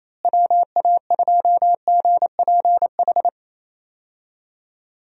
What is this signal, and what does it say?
Morse code, heard as a keyed tone, about 28 wpm, 700 Hz. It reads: WA2GPH
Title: USA Call Signs - 6 Characters - Repeated 3x - 28wpm